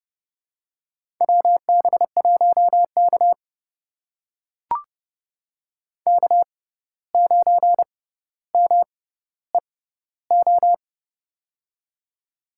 Morse code, heard as a keyed tone, about 31 wpm, spaced as for 15 wpm, 700 Hz. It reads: WB1K K9MEO
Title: USA Call Signs - 6x Character Spacing - 30wpm